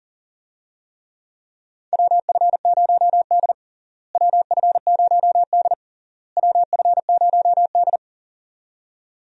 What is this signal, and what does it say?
Morse code, heard as a keyed tone, about 40 wpm, 700 Hz. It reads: WF0D WF0D WF0D
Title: USA Call Signs - 4 Characters - Repeated 3x - 40wpm